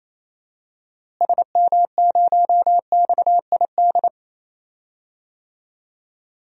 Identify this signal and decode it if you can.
Morse code, heard as a keyed tone, about 28 wpm, 700 Hz. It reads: SM0XID
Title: DXCC-100 Call Signs - 6 Characters - Repeated 2x - 28wpm